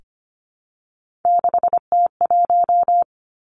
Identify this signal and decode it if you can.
Morse code, heard as a keyed tone, about 25 wpm, 700 Hz. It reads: 6T1
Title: Three Challenging Letter-Number Combinations 25wpm